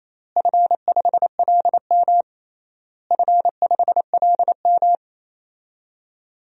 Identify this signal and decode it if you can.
Morse code, heard as a keyed tone, about 28 wpm, 700 Hz. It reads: F5LM F5LM
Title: DXCC-100 Call Signs - 4 Characters - Repeated 3x - 28wpm